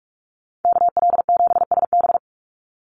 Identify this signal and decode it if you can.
Morse code, heard as a keyed tone, about 44 wpm, 700 Hz. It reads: KL7SB
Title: Call Signs Repeated 2x - 45wpm